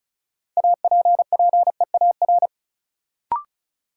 Morse code, heard as a keyed tone, about 36 wpm, 700 Hz. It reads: APPEAR
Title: Sentences from Top 1000 Words - Review Part 2 - New Word Drill - 35wpm